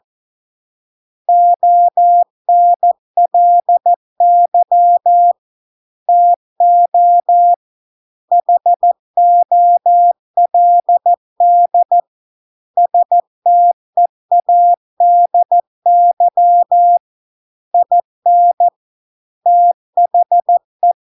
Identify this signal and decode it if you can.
Morse code, heard as a keyed tone, about 14 wpm, 700 Hz. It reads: ONLY TO HOLD STEADY IN THE